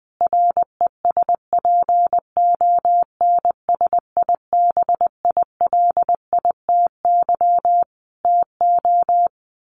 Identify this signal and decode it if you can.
Morse code, heard as a keyed tone, about 20 wpm, 700 Hz. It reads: RESPONSIBILITY TO
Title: Sentences from Top 1200 Words 20wpm